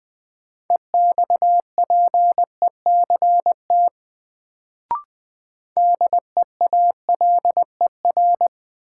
Morse code, heard as a keyed tone, about 21 wpm, 700 Hz. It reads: EXPECT DEALER
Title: Sentences from Top 1100 Words - New Word Drill Alt - 20wpm